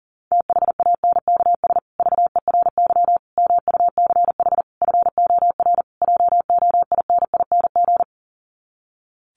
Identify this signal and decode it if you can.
Morse code, heard as a keyed tone, about 37 wpm, 700 Hz. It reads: THANKS VERY MUCH FOR JOINING
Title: Sets of 5 Words Encore2 40wpm